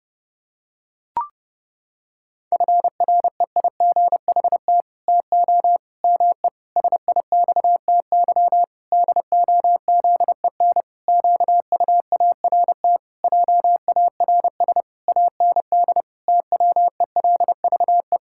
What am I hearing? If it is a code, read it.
Morse code, heard as a keyed tone, about 30 wpm, 700 Hz. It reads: FREIGHT TO ME SIXTY DOZEN QUART JARS AND TWELVE
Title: Pangram Challenge v2 - 30wpm